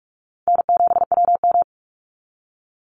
Morse code, heard as a keyed tone, about 44 wpm, 700 Hz. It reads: N7WM
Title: NA SOTA Chaser Callsigns 45wpm